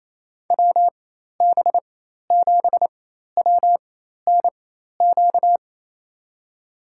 Morse code, heard as a keyed tone, about 28 wpm, spaced as for 13 wpm, 700 Hz. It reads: WB7WNQ
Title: USA Call Signs - 6 Characters - 4x Character Spacing - 28wpm